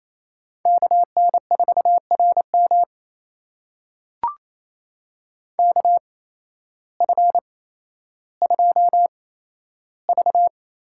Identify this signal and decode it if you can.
Morse code, heard as a keyed tone, about 28 wpm, spaced as for 13 wpm, 700 Hz. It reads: KN4RM KF2V
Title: USA Call Signs - 8x Character Spacing - 28wpm